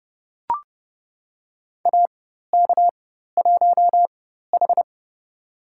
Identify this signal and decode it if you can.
Morse code, heard as a keyed tone, about 30 wpm, spaced as for 14 wpm, 700 Hz. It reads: AK1H
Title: USA Call Signs - 4x Character Spacing - 30wpm